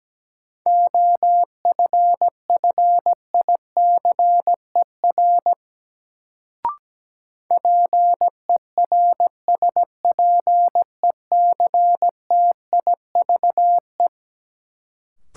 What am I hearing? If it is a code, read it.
Morse code, heard as a keyed tone, about 17 wpm, 700 Hz. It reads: OFFICER PERSPECTIVE
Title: Sentences from Top 1200 Words - New Word Drill - 17wpm